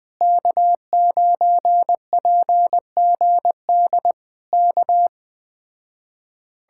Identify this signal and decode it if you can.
Morse code, heard as a keyed tone, about 20 wpm, 700 Hz. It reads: K9PGD K
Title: Speed-Racing Common QSO Phrases Encore - 17wpm